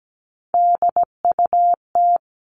Morse code, heard as a keyed tone, about 17 wpm, 700 Hz. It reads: DUT
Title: All Cut Numbers - 3-Digit 17wpm